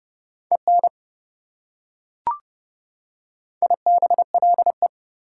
Morse code, heard as a keyed tone, about 28 wpm, 700 Hz. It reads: EN IBLE